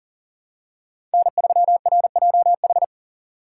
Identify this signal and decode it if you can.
Morse code, heard as a keyed tone, about 40 wpm, 700 Hz. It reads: N3RJH
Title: USA Call Signs - 5 Characters - 4x Character Spacing - 40wpm